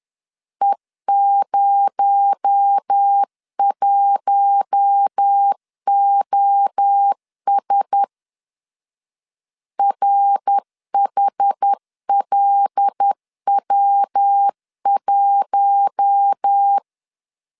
Morse code, heard as a keyed tone, about 10 wpm, 800 Hz. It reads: E01OS RHLW1
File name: CW / DET / 1 6 1500C S B 10wpm 4137